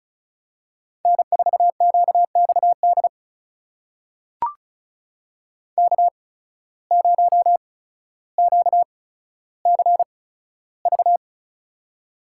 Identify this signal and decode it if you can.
Morse code, heard as a keyed tone, about 36 wpm, spaced as for 15 wpm, 700 Hz. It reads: N4QXD K0QCV
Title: USA Call Signs - 8x Character Spacing - 35wpm